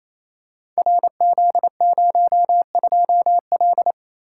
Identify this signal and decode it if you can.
Morse code, heard as a keyed tone, about 29 wpm, 700 Hz. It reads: RZ02L
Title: Code Groups - 5 Letter:Numbers - 28wpm